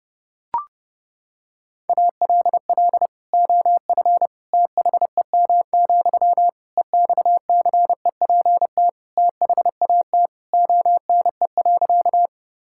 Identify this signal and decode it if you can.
Morse code, heard as a keyed tone, about 32 wpm, 700 Hz. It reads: ALL OF THEM, EXCEPT THAT ONE.
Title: Sentences from Top 1600 Words 30wpm